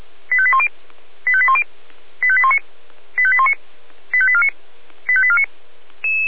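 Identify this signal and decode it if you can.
Five-tone selective call (ZVEI), heard as quick runs of five tones, tones of about 70 ms. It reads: 85810 85810 85819 85819 85849 85849
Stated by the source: ZVEI five-tone selective call received on an SDR.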